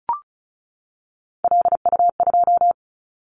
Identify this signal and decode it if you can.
Morse code, heard as a keyed tone, about 34 wpm, 700 Hz. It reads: LU2
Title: Callsign Prefixes for Top 100 DXCC - 35wpm